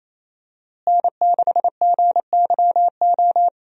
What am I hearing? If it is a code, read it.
Morse code, heard as a keyed tone, about 28 wpm, 700 Hz. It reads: N6GYO
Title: USA Call Signs - 5 Characters - Repeated 3x - 28wpm